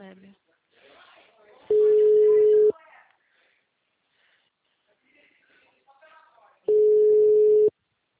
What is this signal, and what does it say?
Ringback tone (Europe, 425 Hz), with speech.